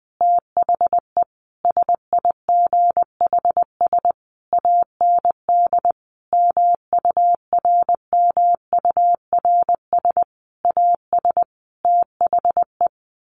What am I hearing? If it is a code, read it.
Morse code, heard as a keyed tone, about 20 wpm, 700 Hz. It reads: THE SIGHS AND MURMURS AS THE
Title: A Sound Of Thunder 20wpm